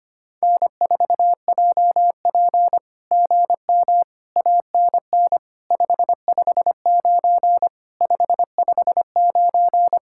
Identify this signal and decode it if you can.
Morse code, heard as a keyed tone, about 25 wpm, 700 Hz. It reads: N4JP GM ANN 559 559
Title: SOTA QSO Practice 25wpm